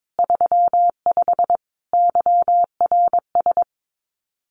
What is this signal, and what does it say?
Morse code, heard as a keyed tone, about 22 wpm, 700 Hz. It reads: 35 YRS
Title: Common QSO Phrases Encore 22wpm